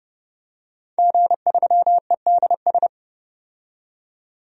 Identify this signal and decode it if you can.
Morse code, heard as a keyed tone, about 29 wpm, 700 Hz. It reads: G3EDS
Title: DXCC-100 Call Signs - 5 Characters - 30wpm